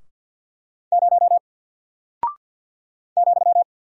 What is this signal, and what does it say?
Morse code, heard as a keyed tone, about 50 wpm, 700 Hz. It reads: 0 ,
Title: Single Character (Letter:Number:Punctuation:Prosign) Alt 50wpm